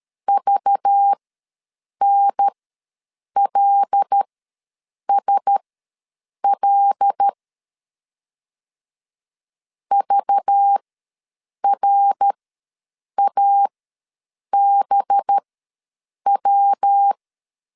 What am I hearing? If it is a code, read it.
Morse code, heard as a keyed tone, about 13 wpm, spaced as for 7 wpm, 800 Hz. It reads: VNLSL VRABW
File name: LEZIONI / 5 LBVC / 5 LBV ws 12wpm 3015